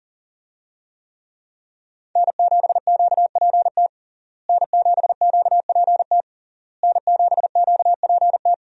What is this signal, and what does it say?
Morse code, heard as a keyed tone, about 40 wpm, 700 Hz. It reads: N7QPT N7QPT N7QPT
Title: USA Call Signs - 5 Characters - Repeated 3x - 40wpm